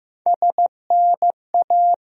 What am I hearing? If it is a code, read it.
Morse code, heard as a keyed tone, about 16 wpm, 700 Hz. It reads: SNA
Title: USA Call Sign Postfixes - 3 Letters 15wpm